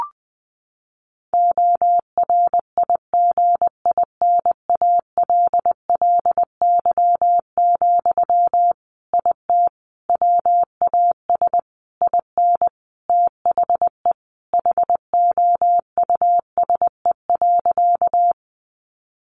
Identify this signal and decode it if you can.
Morse code, heard as a keyed tone, about 20 wpm, 700 Hz. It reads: ORIGINALLY, IT WAS IN THE HOUSE.
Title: Sentences from Top 1500 Words 20wpm